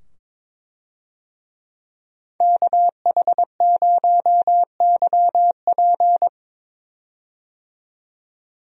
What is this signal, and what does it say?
Morse code, heard as a keyed tone, about 22 wpm, 700 Hz. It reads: KH0YP
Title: USA Call Signs - 5 Characters - 22wpm